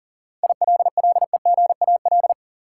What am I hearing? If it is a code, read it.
Morse code, heard as a keyed tone, about 37 wpm, 700 Hz. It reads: ILLEGAL
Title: Seven Letter Words - 40wpm